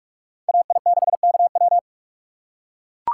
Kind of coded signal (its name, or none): Morse code